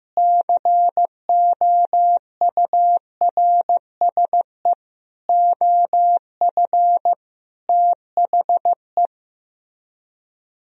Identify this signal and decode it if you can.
Morse code, heard as a keyed tone, about 15 wpm, 700 Hz. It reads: COURSE OF THE
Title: Speed-Racing - Sets of 5 Words 15wpm